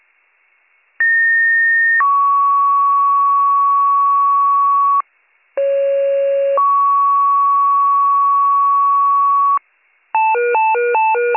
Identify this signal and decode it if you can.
Two-tone page, heard as a one-second tone, then a three-second tone, about 1850 then 1120 Hz, about 550 then 1120 Hz. It is heard from 1.0 to 9.6 s.